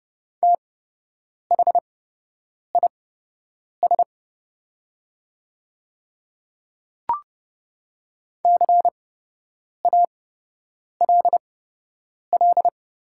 Morse code, heard as a keyed tone, about 30 wpm, spaced as for 8 wpm, 700 Hz. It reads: THIS CALL
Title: 8x Character Spacing - Top 100 Words - 30wpm